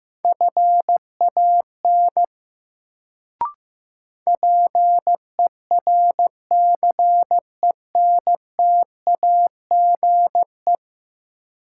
Morse code, heard as a keyed tone, about 15 wpm, 700 Hz. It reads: FAN PERCENTAGE
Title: Sentences from Top 1300 Words - New Word Drill Alt - 15wpm